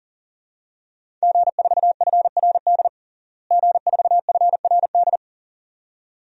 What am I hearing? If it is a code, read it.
Morse code, heard as a keyed tone, about 40 wpm, 700 Hz. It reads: G4FRD G4FRD
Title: DXCC-100 Call Signs - 5 Characters - Repeated 2x - 40wpm